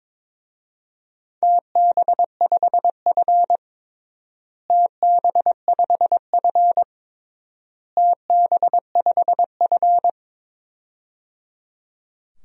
Morse code, heard as a keyed tone, about 22 wpm, 700 Hz. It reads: TB5F TB5F TB5F
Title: DXCC-100 Call Signs - 4 Characters - Repeated 3x - 22wpm